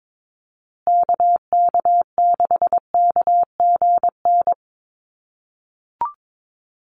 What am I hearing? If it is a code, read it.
Morse code, heard as a keyed tone, about 22 wpm, 700 Hz. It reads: KK6KGN
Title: USA Call Signs Repeated 3x 22wpm